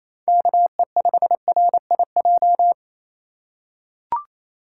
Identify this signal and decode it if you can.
Morse code, heard as a keyed tone, about 28 wpm, 700 Hz. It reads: KE5RIJ
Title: USA Call Signs - 6 Characters - 3x Character Spacing - 28wpm